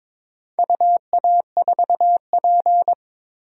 Morse code, heard as a keyed tone, about 22 wpm, 700 Hz. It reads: UA4P
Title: DXCC-100 Call Signs - 4 Characters - 22wpm